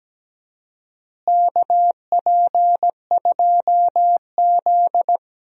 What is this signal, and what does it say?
Morse code, heard as a keyed tone, about 17 wpm, 700 Hz. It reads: KP2Z
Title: DXCC-100 Call Signs - 4 Characters - Repeated 3x - 17wpm